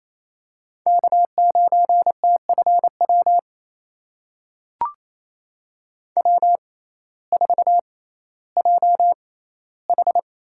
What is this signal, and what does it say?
Morse code, heard as a keyed tone, about 28 wpm, spaced as for 15 wpm, 700 Hz. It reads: K9TFW W4JH
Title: USA Call Signs - 6x Character Spacing - 28wpm